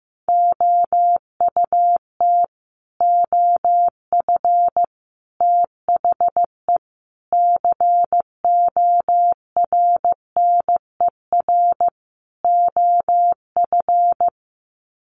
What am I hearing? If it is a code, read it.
Morse code, heard as a keyed tone, about 15 wpm, 700 Hz. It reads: OUT OF THE CORNER OF
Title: Sets of 5 Words 15wpm